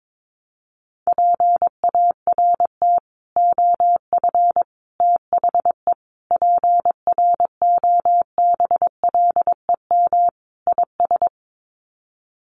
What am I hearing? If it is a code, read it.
Morse code, heard as a keyed tone, about 22 wpm, 700 Hz. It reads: PART OF THE PROBLEM IS
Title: Sets of 5 Words Encore2 22wpm